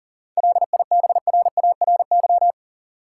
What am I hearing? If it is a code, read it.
Morse code, heard as a keyed tone, about 40 wpm, 700 Hz. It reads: LIBRARY
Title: Sentences from Top 600 Words - New Word Drill - 40wpm